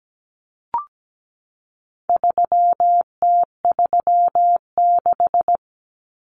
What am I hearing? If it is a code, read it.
Morse code, heard as a keyed tone, about 17 wpm, 700 Hz. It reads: 3T36
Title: Common Cut Numbers - 4-Digit 17wpm